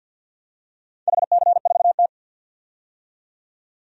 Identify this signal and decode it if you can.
Morse code, heard as a keyed tone, about 51 wpm, 700 Hz. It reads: HC4T
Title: DXCC-100 Call Signs - 4 Characters - 50wpm